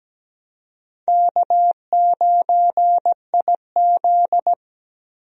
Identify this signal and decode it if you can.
Morse code, heard as a keyed tone, about 17 wpm, 700 Hz. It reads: K9IZ
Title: USA Call Signs - 4 Characters - Repeated 2x - 17wpm